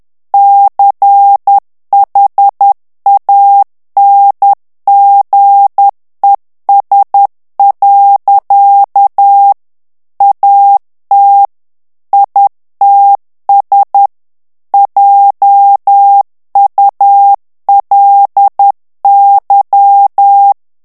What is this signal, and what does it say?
Morse code, heard as a keyed tone, about 10 wpm, 800 Hz. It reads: CHANGES. AT ITS JULY
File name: ANGLAIS texte clair 3 vitesse 10 mots minute